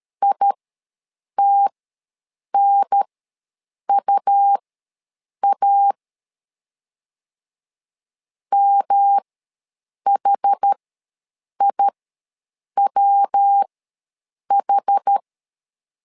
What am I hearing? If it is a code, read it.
Morse code, heard as a keyed tone, about 13 wpm, spaced as for 7 wpm, 800 Hz. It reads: ITNUA MHIWH
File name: LEZIONI / 3 MWRO / 3 MW ws 12wpm 1816